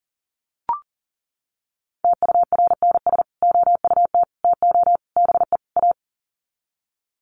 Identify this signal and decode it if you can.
Morse code, heard as a keyed tone, about 37 wpm, 700 Hz. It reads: TURNS OUT TO BE A
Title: Sets of 5 Words Encore 40wpm